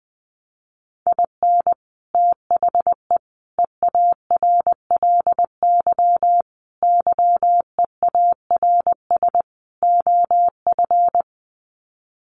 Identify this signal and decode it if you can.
Morse code, heard as a keyed tone, about 20 wpm, 700 Hz. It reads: IN THE EARLY YEARS OF
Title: Sets of 5 Words Encore 20wpm